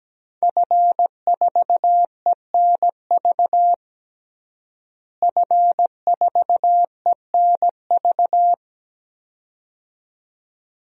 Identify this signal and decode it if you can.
Morse code, heard as a keyed tone, about 17 wpm, 700 Hz. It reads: F4ENV F4ENV
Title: DXCC-100 Call Signs - 5 Characters - Repeated 2x - 17wpm